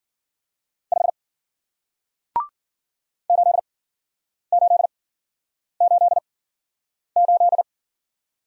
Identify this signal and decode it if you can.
Morse code, heard as a keyed tone, about 50 wpm, spaced as for 9 wpm, 700 Hz. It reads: 5 8888